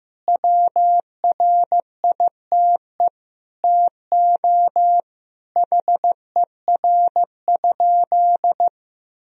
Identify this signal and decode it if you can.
Morse code, heard as a keyed tone, about 15 wpm, 700 Hz. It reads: WRITE TO HER?
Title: Sentences from Top 1000 Words - Review Part 1 - 15wpm